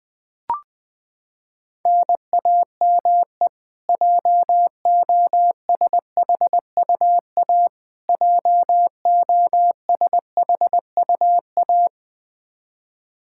Morse code, heard as a keyed tone, about 20 wpm, 700 Hz. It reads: NAME JOSHUA JOSHUA
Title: Practice QSO - Exchange 1 - Beginning - 20wpm